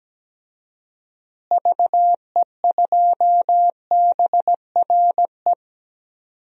Morse code, heard as a keyed tone, about 17 wpm, 700 Hz. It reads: VE2BRE
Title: DXCC-100 Call Signs - 6 Characters - Repeated 3x - 17wpm